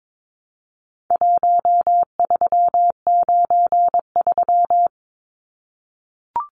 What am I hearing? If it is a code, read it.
Morse code, heard as a keyed tone, about 22 wpm, 700 Hz. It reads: 1393
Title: Numbers - 4-Digit 22wpm